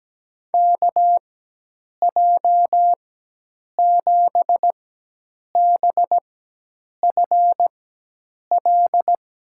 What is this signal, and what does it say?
Morse code, heard as a keyed tone, about 17 wpm, spaced as for 8 wpm, 700 Hz. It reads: KJ7BFL
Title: USA Call Signs - 6 Characters - 4x Character Spacing - 17wpm